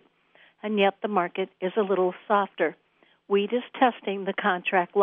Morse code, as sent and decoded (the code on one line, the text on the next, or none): none